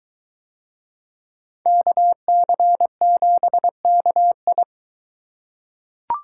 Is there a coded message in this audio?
Yes: Morse code from 1.7 to 4.6 s, heard as a keyed tone, about 21 wpm, 700 Hz. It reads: KC7KI